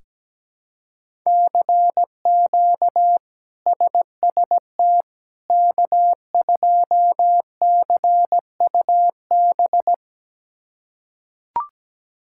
Morse code, heard as a keyed tone, about 17 wpm, 700 Hz. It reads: CQ SST K2CUB
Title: K1USN SST QSO Practice - 17wpm